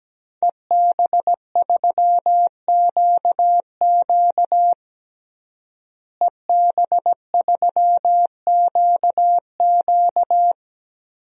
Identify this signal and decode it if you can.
Morse code, heard as a keyed tone, about 17 wpm, 700 Hz. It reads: EB3QQ EB3QQ
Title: DXCC-100 Call Signs - 5 Characters - Repeated 2x - 17wpm